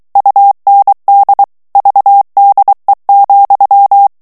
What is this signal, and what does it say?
Morse code, heard as a keyed tone, about 23 wpm, 800 Hz. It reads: UND VDE,
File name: ALLEMAND texte clair 1 vitesse 22 mots minute